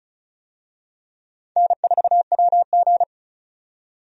Morse code, heard as a keyed tone, about 36 wpm, 700 Hz. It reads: N4WG
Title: USA Call Signs - 4 Characters - 35wpm